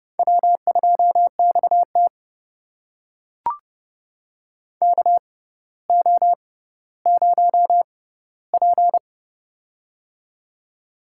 Morse code, heard as a keyed tone, about 30 wpm, spaced as for 15 wpm, 700 Hz. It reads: W2XT KO0P
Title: USA Call Signs - 4 Characters - 6x Character Spacing - 30wpm